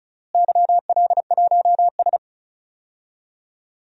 Morse code, heard as a keyed tone, about 34 wpm, 700 Hz. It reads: YL1S
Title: DXCC-100 Call Signs - 4 Characters - Repeated 2x - 35wpm